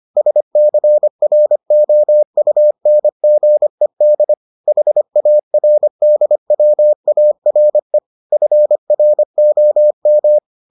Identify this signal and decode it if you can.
Morse code, heard as a keyed tone, about 25 wpm, 600 Hz. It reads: SCROUNGED HARDWARE FROM